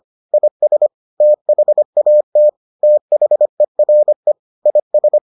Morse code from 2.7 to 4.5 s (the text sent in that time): THERE